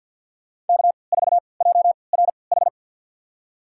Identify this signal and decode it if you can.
Morse code, heard as a keyed tone, about 46 wpm, spaced as for 27 wpm, 700 Hz. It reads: K4JRH
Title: USA Call Signs - 3x Character Spacing - 50wpm